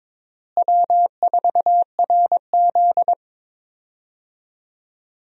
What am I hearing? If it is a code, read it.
Morse code, heard as a keyed tone, about 22 wpm, 700 Hz. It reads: W4RZ
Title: USA Call Signs - 4 Characters - Repeated 3x - 22wpm